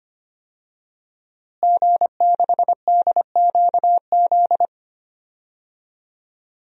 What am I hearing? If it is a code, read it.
Morse code, heard as a keyed tone, about 25 wpm, 700 Hz. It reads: G6DQZ
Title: DXCC-100 Call Signs - 5 Characters - Repeated 2x - 25wpm